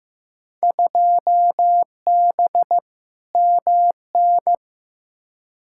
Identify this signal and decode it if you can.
Morse code, heard as a keyed tone, about 15 wpm, 700 Hz. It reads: 2B MN
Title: Field Day Exchanges - 15wpm